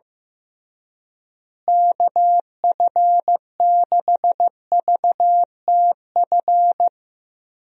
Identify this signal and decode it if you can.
Morse code, heard as a keyed tone, about 15 wpm, 700 Hz. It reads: KF6VTF
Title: USA Call Signs - 6 Characters - Repeated 3x - 15wpm